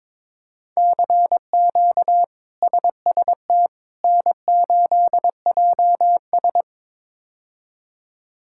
Morse code, heard as a keyed tone, about 21 wpm, 700 Hz. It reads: CQ SST N8JS
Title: K1USN SST QSO Practice - 22wpm